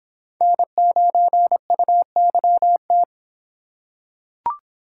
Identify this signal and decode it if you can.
Morse code, heard as a keyed tone, about 23 wpm, 700 Hz. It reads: N9UYT